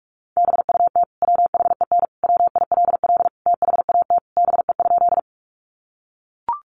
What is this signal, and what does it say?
Morse code, heard as a keyed tone, about 44 wpm, 700 Hz. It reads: BUT WHEN WILL THAT BE?
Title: Sentences from Top 100 Words 45wpm